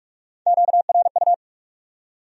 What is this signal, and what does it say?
Morse code, heard as a keyed tone, about 56 wpm, 700 Hz. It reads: QRU
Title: Speed-Racing Q-Codes - 45wpm